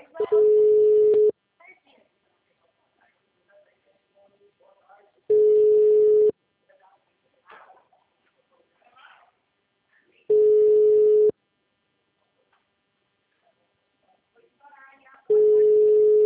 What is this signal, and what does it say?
Ringback tone (Europe, 425 Hz), with speech.